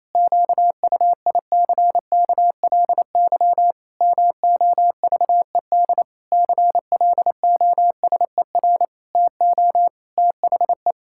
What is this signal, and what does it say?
Morse code, heard as a keyed tone, about 28 wpm, 700 Hz. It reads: QUICKLY MOVED CLOSER TO THE
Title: Pangram Challenge v2 - 28wpm